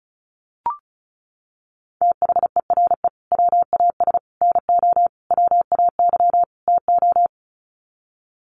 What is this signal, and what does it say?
Morse code, heard as a keyed tone, about 34 wpm, 700 Hz. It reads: THERE WAS NO WAY TO
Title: Sets of 5 Words Encore 35wpm